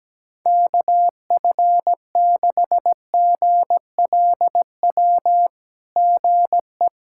Morse code, heard as a keyed tone, about 17 wpm, 700 Hz. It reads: KF6GLW GE
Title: SOTA QSO Practice 17wpm